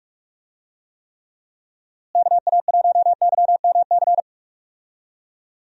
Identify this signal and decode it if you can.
Morse code, heard as a keyed tone, about 45 wpm, 700 Hz. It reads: KA1YMC
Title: USA Call Signs - 6 Characters - Repeated 3x - 45wpm